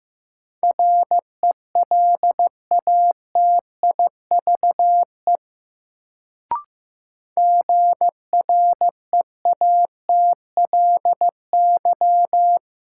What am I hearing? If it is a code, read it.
Morse code, heard as a keyed tone, about 15 wpm, 700 Hz. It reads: RELATIVE GREATLY
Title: Sentences from Top 1400 Words - New Word Drill Alt - 15wpm